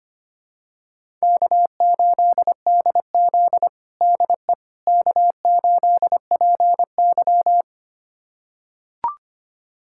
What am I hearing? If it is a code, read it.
Morse code, heard as a keyed tone, about 25 wpm, 700 Hz. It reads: K8DZ DE K8PY
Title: USA Callsign DE Callsign - 4 Characters - 25wpm